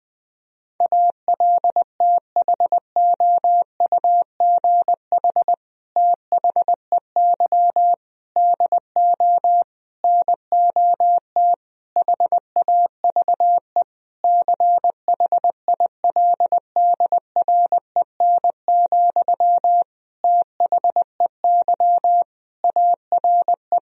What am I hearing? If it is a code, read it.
Morse code, heard as a keyed tone, about 20 wpm, 700 Hz. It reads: ALTHOUGH THEY DO NOT HAVE CHILDREN, THEY ARE
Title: Sentences from Top 1000 Words - Review Part 3 - 20wpm